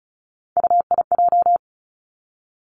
Morse code, heard as a keyed tone, about 34 wpm, 700 Hz. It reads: UIJ
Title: Three Challenging Letter-Number Combinations 35wpm